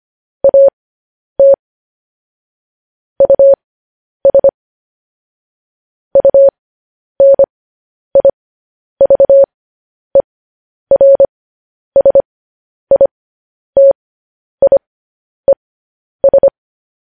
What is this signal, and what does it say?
Morse code, heard as a keyed tone, about 25 wpm, spaced as for 10 wpm, 550 Hz. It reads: AT US UNIVERSITIES
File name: podcast / News headlines-1x-25x10